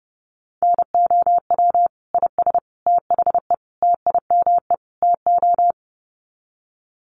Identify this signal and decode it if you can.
Morse code, heard as a keyed tone, about 30 wpm, 700 Hz. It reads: NOW IS THE TIME TO
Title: Sets of 5 Words Encore2 30wpm